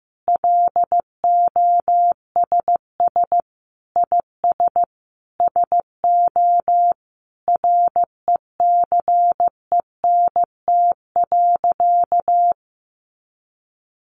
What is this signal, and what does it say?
Morse code, heard as a keyed tone, about 14 wpm, 700 Hz. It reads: LOSS IS SO RECENT.
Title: Sentences from Top 1400 Words 15wpm